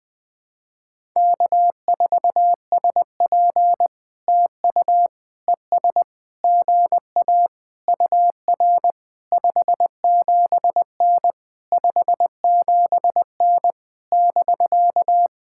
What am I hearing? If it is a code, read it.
Morse code, heard as a keyed tone, about 20 wpm, 700 Hz. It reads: K4SP TU ES GA UR 57N 57N <BK>
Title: POTA QSO Practice - 20wpm